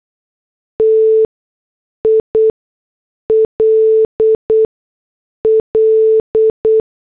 Morse code, heard as a keyed tone, about 8 wpm, 440 Hz. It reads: T I L L